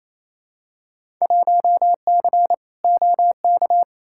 Morse code, heard as a keyed tone, about 28 wpm, 700 Hz. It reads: 1C OK
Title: Field Day Exchanges Encore - 28wpm